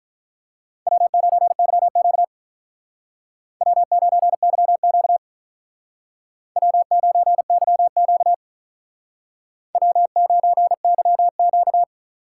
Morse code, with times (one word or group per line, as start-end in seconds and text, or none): 0.9-2.2 W9YQ
3.6-5.2 W9YQ
6.6-8.3 W9YQ
9.7-11.8 W9YQ